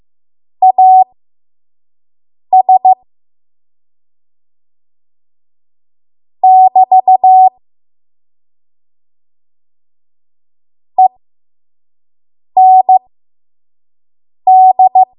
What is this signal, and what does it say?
Morse code, heard as a keyed tone, about 15 wpm, spaced as for 5 wpm, 750 Hz. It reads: AS <BT> END
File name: Morse / 05WPM